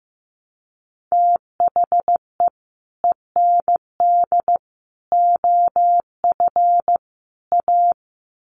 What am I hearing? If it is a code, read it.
Morse code, heard as a keyed tone, about 15 wpm, 700 Hz. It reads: THE END OF A
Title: Sets of 4 Words 15wpm